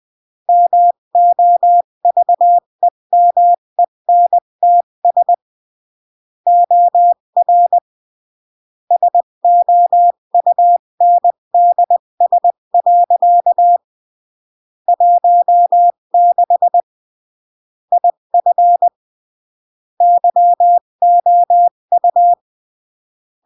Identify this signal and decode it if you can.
Morse code, heard as a keyed tone, about 20 wpm, 700 Hz. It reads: MOVEMENTS OR SOUNDS. 16 IF YOU
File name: CW / jodorowsky / 82 commandments 20-15